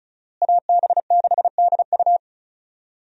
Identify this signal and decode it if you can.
Morse code, heard as a keyed tone, about 36 wpm, 700 Hz. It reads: AB6DU
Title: USA Call Signs - 5 Characters - Repeated 2x - 35wpm